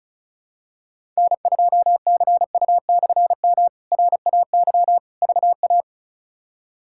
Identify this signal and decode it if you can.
Morse code, heard as a keyed tone, about 36 wpm, 700 Hz. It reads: N2CU/M RAY VA
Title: K1USN SST QSO Practice - 35wpm